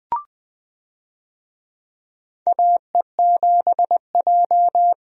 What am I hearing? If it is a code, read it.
Morse code, heard as a keyed tone, about 20 wpm, 700 Hz. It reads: AE7J
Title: USA Call Signs - 4 Characters - Repeated 2x - 20wpm